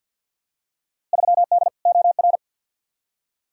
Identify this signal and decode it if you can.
Morse code, heard as a keyed tone, about 49 wpm, 700 Hz. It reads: 3D OR
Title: Field Day Exchanges - 50wpm